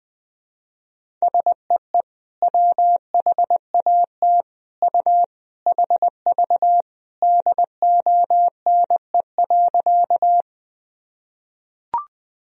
Morse code, heard as a keyed tone, about 20 wpm, 700 Hz. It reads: SEE WHAT U HV DONE.
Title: Sentences from Top 500 Words and Abbreviations 20wpm